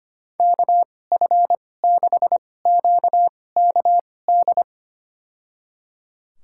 Morse code, heard as a keyed tone, about 25 wpm, spaced as for 18 wpm, 700 Hz. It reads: KF6QKD
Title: USA Call Signs - 6 Characters - 2x Character Spacing - 25wpm